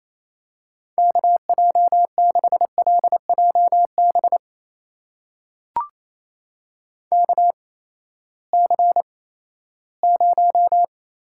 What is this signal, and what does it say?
Morse code, heard as a keyed tone, about 28 wpm, spaced as for 16 wpm, 700 Hz. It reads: KJ6LJB KC0
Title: USA Call Signs - 6 Characters - 8x Character Spacing - 28wpm